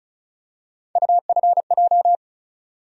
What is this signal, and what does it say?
Morse code, heard as a keyed tone, about 36 wpm, 700 Hz. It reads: UFJ
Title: USA Call Sign Postfixes - 3 Letters 35wpm